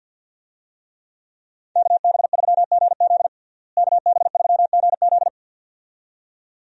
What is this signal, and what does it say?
Morse code, heard as a keyed tone, about 51 wpm, 700 Hz. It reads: KB3GZ KB3GZ
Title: USA Call Signs - 5 Characters - Repeated 2x - 50wpm